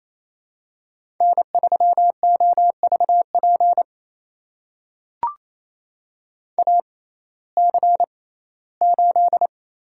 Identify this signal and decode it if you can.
Morse code, heard as a keyed tone, about 27 wpm, spaced as for 17 wpm, 700 Hz. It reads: N3OVP AC8
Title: USA Call Signs - 5 Characters - 6x Character Spacing - 28wpm